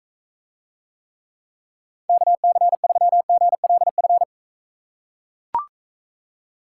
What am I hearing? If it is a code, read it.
Morse code, heard as a keyed tone, about 40 wpm, 700 Hz. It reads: KC3GLF